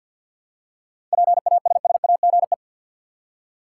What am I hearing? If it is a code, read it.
Morse code, heard as a keyed tone, about 50 wpm, 700 Hz. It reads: PASSAGE
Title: Seven Letter Words - 50wpm